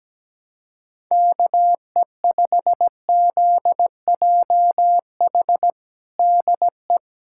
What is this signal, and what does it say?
Morse code, heard as a keyed tone, about 17 wpm, 700 Hz. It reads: KE5ZJH DE
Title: USA Callsign DE Callsign - 6 Characters - 17wpm